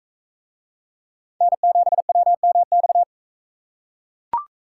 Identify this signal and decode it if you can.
Morse code, heard as a keyed tone, about 42 wpm, 700 Hz. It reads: N7WMX